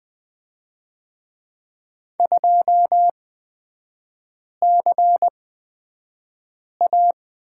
Morse code, heard as a keyed tone, about 20 wpm, spaced as for 5 wpm, 690 Hz. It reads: 2CA
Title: 255 Lesson 41e - Period - Three Characters 20wpm